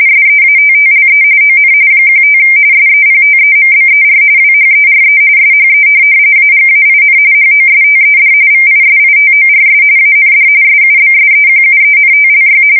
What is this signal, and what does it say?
RTTY, about 45.45 baud. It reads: SEATTLE ON 15 SEPTEMBER WHERE SHE AGAIN HOSTED VISITORS AND SUBSEQUENTLY
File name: fsk170 / USS-Missouri